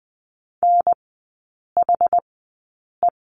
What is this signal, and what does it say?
Morse code, heard as a keyed tone, about 20 wpm, spaced as for 8 wpm, 700 Hz. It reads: NHE
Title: 074 Lesson 13e - Letter L - Three Characters 20wpm